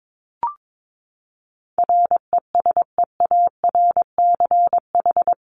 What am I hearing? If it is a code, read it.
Morse code, heard as a keyed tone, about 22 wpm, 700 Hz. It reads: RESEARCH
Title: Top 501-750 Words 22wpm